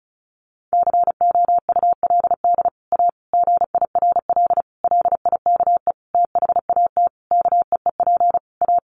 Morse code, heard as a keyed tone, about 34 wpm, 700 Hz. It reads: COULD A GIRL LIKE THAT KEEP A
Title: Sentences from Top 1200 Words 35wpm